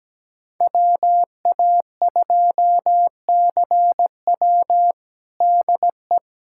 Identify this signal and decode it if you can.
Morse code, heard as a keyed tone, about 17 wpm, 700 Hz. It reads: WA2CW DE
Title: USA Callsign DE Callsign - 5 Characters - 17wpm